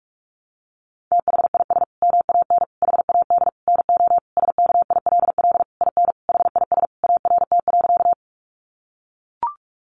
Morse code, heard as a keyed tone, about 44 wpm, 700 Hz. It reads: THIS MAN HAD NO SKILL IN HIS ART.
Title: Sentences from Top 1200 Words 45wpm